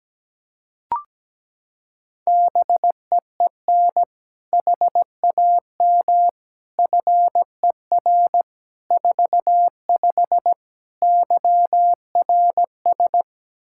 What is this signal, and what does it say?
Morse code, heard as a keyed tone, about 17 wpm, 700 Hz. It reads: BEEN HAM FER 45 YRS
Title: Practice QSO - Exchange 3 - More Info - 17wpm